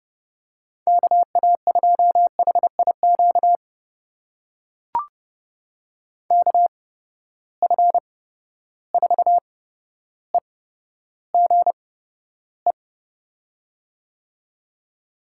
Morse code, heard as a keyed tone, about 31 wpm, spaced as for 13 wpm, 700 Hz. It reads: KA2HIQ KF4EGE
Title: USA Call Signs - 8x Character Spacing - 30wpm